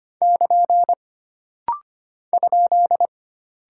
Morse code, heard as a keyed tone, about 25 wpm, 700 Hz. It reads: <KN> ?
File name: Single Punctuation-Prosign - Rapid-Fire - 25wpm